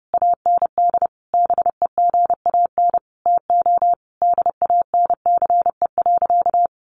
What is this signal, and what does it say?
Morse code, heard as a keyed tone, about 30 wpm, 700 Hz. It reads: AND BEGAN TO DANCE.
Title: Sentences from Top 900 Words 30wpm